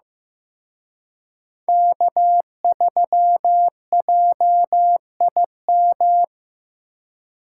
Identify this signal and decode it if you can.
Morse code, heard as a keyed tone, about 18 wpm, 700 Hz. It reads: K3JIM
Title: Speed Racing - USA 1x3 Call Signs 15wpm